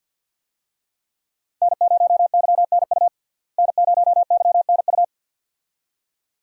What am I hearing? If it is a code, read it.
Morse code, heard as a keyed tone, about 52 wpm, 700 Hz. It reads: N0YNU N0YNU
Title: USA Call Signs - 5 Characters - Repeated 2x - 50wpm